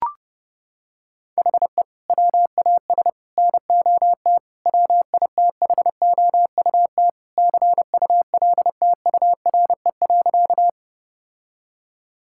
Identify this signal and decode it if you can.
Morse code, heard as a keyed tone, about 30 wpm, 700 Hz. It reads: HE WAS NOT WITHOUT CULTURE.
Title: Sentences from Top 1000 Words - Review Part 6 - 30wpm